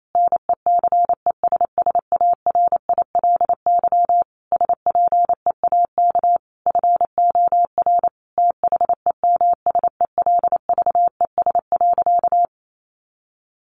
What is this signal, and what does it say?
Morse code, heard as a keyed tone, about 28 wpm, 700 Hz. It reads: NECESSARILY SPEAK FOR THEMSELVES.
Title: Sentences from Top 1400 Words 28wpm